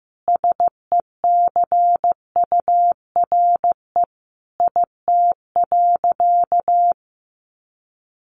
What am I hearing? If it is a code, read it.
Morse code, heard as a keyed tone, about 15 wpm, 700 Hz. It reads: SECURE IT.
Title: Sentences from Top 1200 Words 15wpm